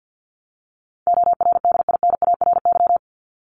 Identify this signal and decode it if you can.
Morse code, heard as a keyed tone, about 50 wpm, 700 Hz. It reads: ORDINARY
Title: Top 1751-2000 Words 50wpm